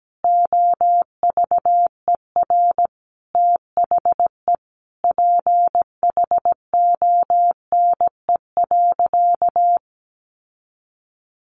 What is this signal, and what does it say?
Morse code, heard as a keyed tone, about 17 wpm, 700 Hz. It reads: OVER THE PHONE.
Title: Sentences from Top 1100 Words 17wpm